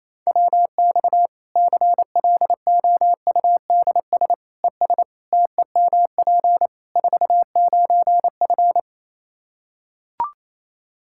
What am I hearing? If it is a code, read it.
Morse code, heard as a keyed tone, about 28 wpm, 700 Hz. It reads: WX CLOUDS ES TEMP 49F
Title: K1USN SST QSO Practice 28wpm